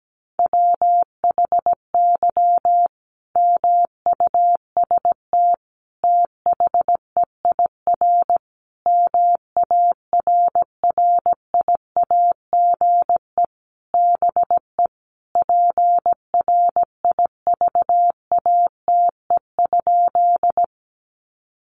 Morse code, heard as a keyed tone, about 17 wpm, 700 Hz. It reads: WHY MUST THEIR MARRIAGE BE PRIVATE?
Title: Sentences from Top 1000 Words 17wpm